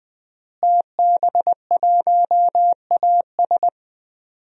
Morse code, heard as a keyed tone, about 21 wpm, 700 Hz. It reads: TB1AS
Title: DXCC-100 Call Signs - 5 Characters - Repeated 2x - 20wpm